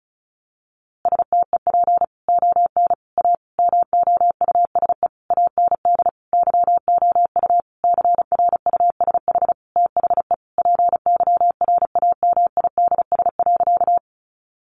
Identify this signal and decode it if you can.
Morse code, heard as a keyed tone, about 34 wpm, 700 Hz. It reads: STEP ON A MOUSE AND YOU CRUSH THE PYRAMIDS.
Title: A Sound Of Thunder 35wpm